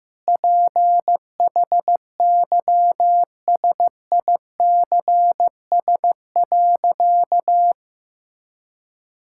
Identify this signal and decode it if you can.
Morse code, heard as a keyed tone, about 15 wpm, 700 Hz. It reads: PHYSICS.
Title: Sentences from Top 1000 Words - Review Part 5 - 15wpm